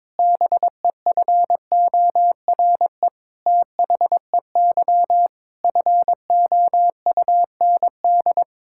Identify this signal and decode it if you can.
Morse code, heard as a keyed tone, about 21 wpm, 700 Hz. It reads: BEFORE THEY FOUND
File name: practice / 22wpm / Sentences with 8-Word Phrases - 22wpm